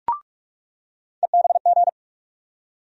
Morse code, heard as a keyed tone, about 46 wpm, 700 Hz. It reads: EBG
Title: USA Call Sign Postfixes - 3 Letters 45wpm